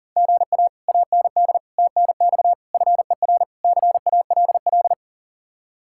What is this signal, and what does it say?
Morse code, heard as a keyed tone, about 52 wpm, 700 Hz. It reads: GA AND TNX FER CALL